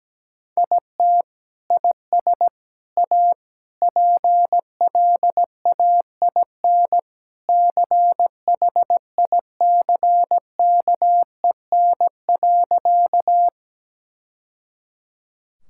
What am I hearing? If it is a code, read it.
Morse code, heard as a keyed tone, about 17 wpm, 700 Hz. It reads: IT IS A PLAIN CHICKEN.
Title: Sentences from Top 1000 Words - Review Part 5 - 17wpm